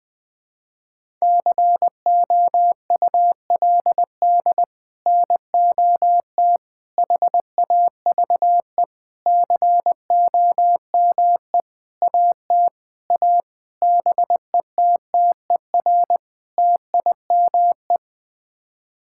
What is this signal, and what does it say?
Morse code, heard as a keyed tone, about 21 wpm, 700 Hz. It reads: COULD NOT HAVE COME AT A BETTER TIME
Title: Sets of 8 Words - 20wpm